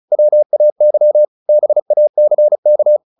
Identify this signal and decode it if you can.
Morse code, heard as a keyed tone, about 34 wpm, 600 Hz. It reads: WAY BACK
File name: the old man and the sea 2 35wpm